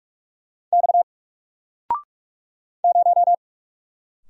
Morse code, heard as a keyed tone, about 46 wpm, 700 Hz. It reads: X 0
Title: Single Letter-Number 45wpm